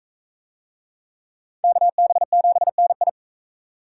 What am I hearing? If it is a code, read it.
Morse code, heard as a keyed tone, about 42 wpm, 700 Hz. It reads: KB7NI